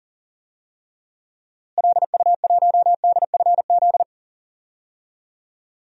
Morse code, heard as a keyed tone, about 39 wpm, 700 Hz. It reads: LU1DFZ
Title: DXCC-100 Call Signs - 6 Characters - Repeated 2x - 40wpm